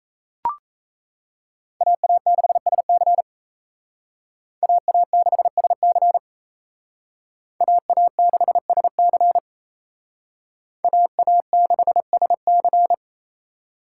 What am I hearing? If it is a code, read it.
Morse code, heard as a keyed tone, about 34 wpm, 700 Hz. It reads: AA6SC AA6SC AA6SC AA6SC
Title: Speed Racing - USA 2x2 Call Signs 28wpm